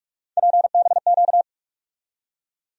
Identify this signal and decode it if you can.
Morse code, heard as a keyed tone, about 44 wpm, 700 Hz. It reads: PBQ
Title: USA Call Sign Postfixes - 3 Letters 45wpm